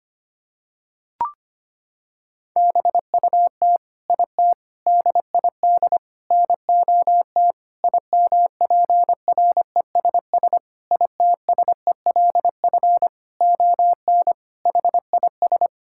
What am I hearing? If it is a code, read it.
Morse code, heard as a keyed tone, about 25 wpm, 700 Hz. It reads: BUT IT DID NOT IMPRESS ITSELF ON HIS
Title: Sentences from Top 1600 Words 25wpm